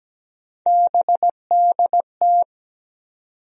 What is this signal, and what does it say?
Morse code, heard as a keyed tone, about 17 wpm, 700 Hz. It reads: BDT
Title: USA Call Sign Postfixes - 3 Letters 17wpm